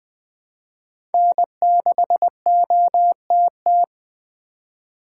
Morse code, heard as a keyed tone, about 20 wpm, 700 Hz. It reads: N6OTT
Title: USA Call Signs - 5 Characters - 2x Character Spacing - 20wpm